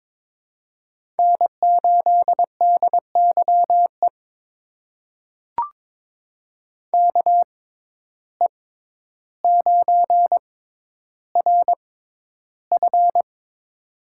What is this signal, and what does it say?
Morse code, heard as a keyed tone, about 22 wpm, spaced as for 11 wpm, 700 Hz. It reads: N8DYE KE9RF
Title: USA Call Signs - 5 Characters - 6x Character Spacing - 22wpm